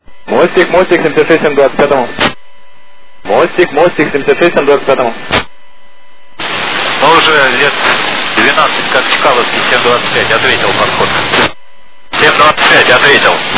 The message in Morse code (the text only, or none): none